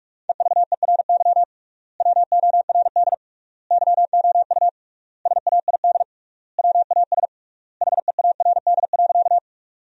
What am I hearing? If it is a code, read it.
Morse code, heard as a keyed tone, about 41 wpm, 700 Hz. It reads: EVERY WORD YOU SAID WAS HEARD.
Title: 3x Word-Spacing - Sentences from Top 400 Words - 45wpm